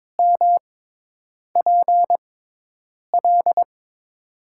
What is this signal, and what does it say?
Morse code, heard as a keyed tone, about 22 wpm, spaced as for 8 wpm, 700 Hz. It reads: MPL
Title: USA Call Signs - 6x Character Spacing - 22wpm